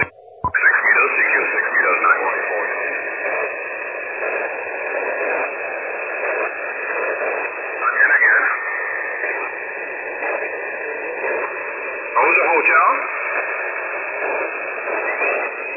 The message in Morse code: none